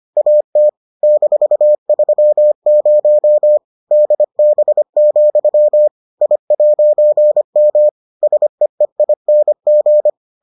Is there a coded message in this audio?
Yes: Morse code from 0.2 to 10.1 s, heard as a keyed tone, about 25 wpm, 600 Hz. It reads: AT -30 DB, I'M SEEING